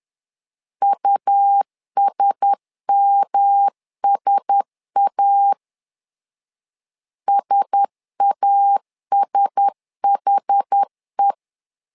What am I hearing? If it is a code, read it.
Morse code, heard as a keyed tone, about 10 wpm, 800 Hz. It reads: USMSA SASHE
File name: LEZIONI / 4 Q1S 10wpm 2133